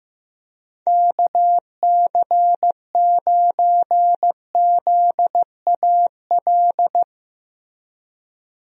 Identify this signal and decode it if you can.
Morse code, heard as a keyed tone, about 15 wpm, 700 Hz. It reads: KC9ZAL
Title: USA Call Signs - 6 Characters - Repeated 2x - 15wpm